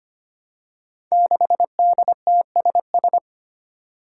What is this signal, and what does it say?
Morse code, heard as a keyed tone, about 25 wpm, 700 Hz. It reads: 6DTSS
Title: Five Challenging Letter-Number Combinations 25wpm